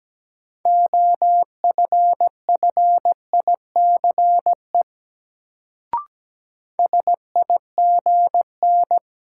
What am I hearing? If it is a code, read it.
Morse code, heard as a keyed tone, about 17 wpm, 700 Hz. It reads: OFFICE SIGN
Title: Sentences from Top 1000 Words - New Word Drill - 17wpm